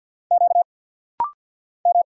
Morse code, heard as a keyed tone, about 50 wpm, 700 Hz. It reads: Q M
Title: Single Letters - Rapid-Fire 50wpm